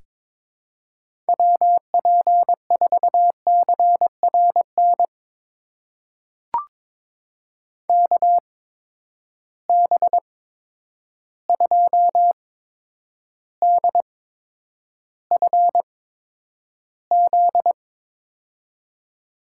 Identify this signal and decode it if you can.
Morse code, heard as a keyed tone, about 22 wpm, spaced as for 9 wpm, 700 Hz. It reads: WP4CRN KB2DFZ
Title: USA Call Signs - 6 Characters - 8x Character Spacing - 22wpm